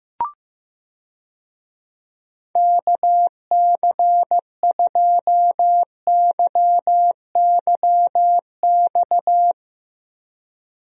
Morse code, heard as a keyed tone, about 15 wpm, 700 Hz. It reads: KC2YYX
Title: USA Call Signs - 6 Characters - Repeated 2x - 15wpm